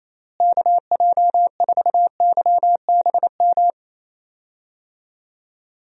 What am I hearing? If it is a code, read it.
Morse code, heard as a keyed tone, about 27 wpm, 700 Hz. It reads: KJ4YBM
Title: USA Call Signs - 6 Characters - Repeated 2x - 28wpm